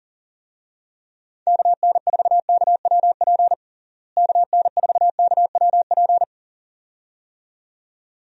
Morse code, heard as a keyed tone, about 40 wpm, 700 Hz. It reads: KN4KWP KN4KWP
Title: USA Call Signs - 6 Characters - Repeated 2x - 40wpm